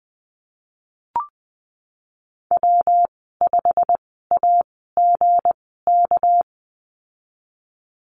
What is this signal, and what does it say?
Morse code, heard as a keyed tone, about 20 wpm, spaced as for 15 wpm, 700 Hz. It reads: W5AGK
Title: 238 Lesson 38g - Letter q - Call Signs 20wpm